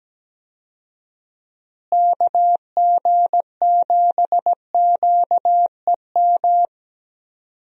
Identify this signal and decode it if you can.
Morse code, heard as a keyed tone, about 17 wpm, 700 Hz. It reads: KG7QEM
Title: USA Call Signs - 6 Characters - Repeated 3x - 17wpm